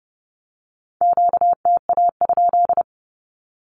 Morse code, heard as a keyed tone, about 30 wpm, 700 Hz. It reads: QTA?
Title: Q-Codes with Spoken Meaning - 30wpm